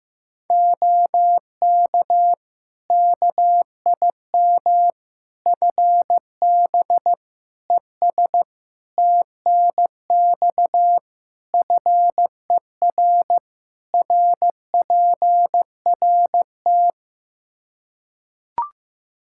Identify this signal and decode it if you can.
Morse code, heard as a keyed tone, about 15 wpm, 700 Hz. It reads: OK KIM FB ES TNX FER RPRT
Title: Practice QSO - Exchange 2 - Info - 15wpm